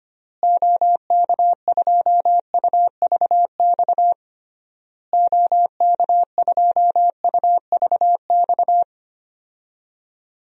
Morse code, heard as a keyed tone, about 25 wpm, 700 Hz. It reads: OK2UVX OK2UVX
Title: DXCC-100 Call Signs - 6 Characters - Repeated 3x - 25wpm